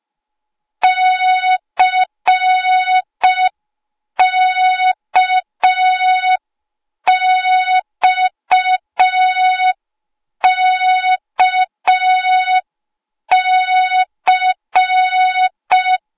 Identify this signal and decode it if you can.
Morse code, heard as a keyed tone, about 5 wpm, 3700 Hz. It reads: CKXKC
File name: Morze-HA2MN / hang / 08KXC HA2MN